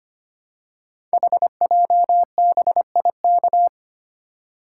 Morse code, heard as a keyed tone, about 25 wpm, 700 Hz. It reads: HJBIK
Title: Code Groups - 5 Letters - 25wpm